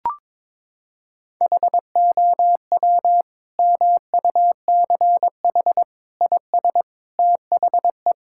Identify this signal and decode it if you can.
Morse code, heard as a keyed tone, about 22 wpm, 700 Hz. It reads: HOW MUCH IS THE
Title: Sentences from Top 1000 Words - Review Part 5 - 22wpm